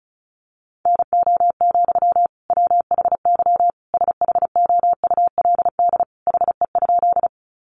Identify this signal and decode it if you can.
Morse code, heard as a keyed tone, about 34 wpm, 700 Hz. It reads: NO, WHY SHOULD HE?
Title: Sentences from Top 300 Words 35wpm